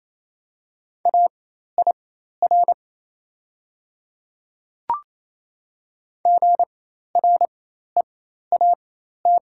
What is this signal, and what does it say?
Morse code, heard as a keyed tone, about 28 wpm, spaced as for 13 wpm, 690 Hz. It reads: AIR GREAT
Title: 4x Character Spacing - Top 300 Words - 28wpm